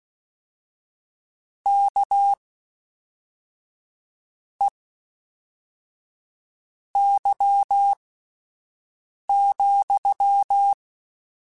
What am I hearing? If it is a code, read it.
Morse code, heard as a keyed tone, about 16 wpm, spaced as for 4 wpm, 800 Hz. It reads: KEY,
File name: Misc / Misc-mp3s / WordPairs sp @5